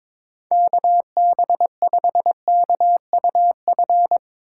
Morse code, heard as a keyed tone, about 22 wpm, 700 Hz. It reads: KB5KUF
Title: Speed Racing - USA 2x3 Call Signs 22wpm